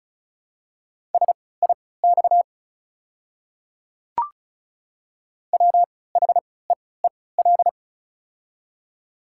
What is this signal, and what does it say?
Morse code, heard as a keyed tone, about 34 wpm, spaced as for 20 wpm, 690 Hz. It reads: SIX WHEEL
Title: 3x Character Spacing - Top 500 Words - 35wpm